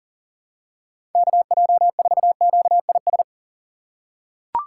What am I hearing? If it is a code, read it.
Morse code, heard as a keyed tone, about 40 wpm, 700 Hz. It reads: KJ4QIS
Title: USA Call Signs - 2x Character Spacing - 40wpm